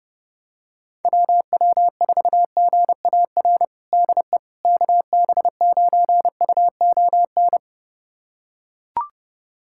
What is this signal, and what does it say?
Morse code, heard as a keyed tone, about 30 wpm, 700 Hz. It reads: WW4GAR DE KB9UON
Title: USA Callsign DE Callsign - 6 Characters - 30wpm